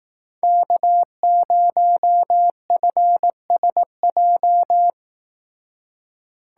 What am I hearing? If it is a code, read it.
Morse code, heard as a keyed tone, about 18 wpm, 700 Hz. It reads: K0FSJ
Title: Speed Racing - USA 1x3 Call Signs 15wpm